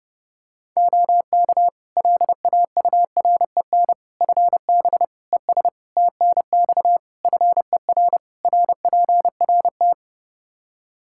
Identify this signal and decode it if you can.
Morse code, heard as a keyed tone, about 30 wpm, 700 Hz. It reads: OK LAUREN FB ES TNX FER RPRT
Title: Practice QSO - Exchange 2 - Info - 30wpm